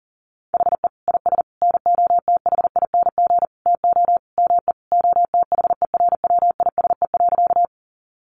Morse code, heard as a keyed tone, about 40 wpm, 700 Hz. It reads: HE IS NOTHING TO ME OTHERWISE.
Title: Sentences from Top 1200 Words 40wpm